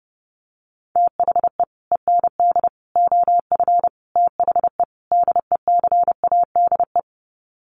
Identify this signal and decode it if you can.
Morse code, heard as a keyed tone, about 30 wpm, 700 Hz. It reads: THE END OF THE DECADE
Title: Sets of 5 Words Encore 30wpm